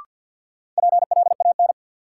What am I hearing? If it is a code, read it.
Morse code, heard as a keyed tone, about 52 wpm, 700 Hz. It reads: PLAN
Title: Sentences from Top 500 Words - New Word Drill - 50wpm